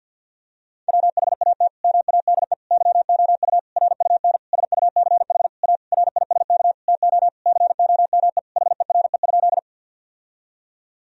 Morse code, heard as a keyed tone, about 50 wpm, 700 Hz. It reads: WHAT MADE YOU RUN SUCH A RISK TO COME HERE?
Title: Sentences from Top 1000 Words - Review Part 4 - 50wpm